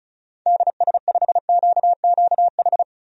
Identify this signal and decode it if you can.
Morse code, heard as a keyed tone, about 34 wpm, 700 Hz. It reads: DS5QQH
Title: DXCC-100 Call Signs - 6 Characters - Repeated 3x - 35wpm